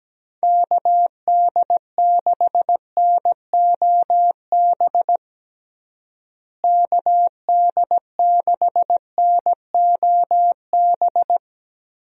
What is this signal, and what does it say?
Morse code, heard as a keyed tone, about 17 wpm, 700 Hz. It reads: KD6NOB KD6NOB
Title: USA Call Signs - 6 Characters - Repeated 3x - 17wpm